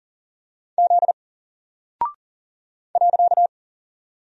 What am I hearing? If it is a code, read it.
Morse code, heard as a keyed tone, about 40 wpm, 700 Hz. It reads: Z .
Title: Single Character (Letter:Number:Punctuation:Prosign) 40wpm